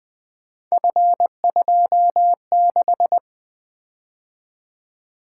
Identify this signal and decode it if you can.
Morse code, heard as a keyed tone, about 20 wpm, 700 Hz. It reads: F26
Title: Three Letter-Number Combinations - 20wpm